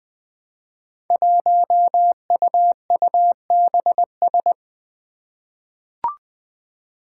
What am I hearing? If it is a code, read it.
Morse code, heard as a keyed tone, about 20 wpm, 700 Hz. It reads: 1UUBS
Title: Five Challenging Letter-Number Combinations 20wpm